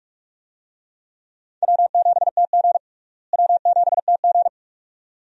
Morse code, heard as a keyed tone, about 46 wpm, 700 Hz. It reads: W7TG W7TG
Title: USA Call Signs - 4 Characters - Repeated 2x - 45wpm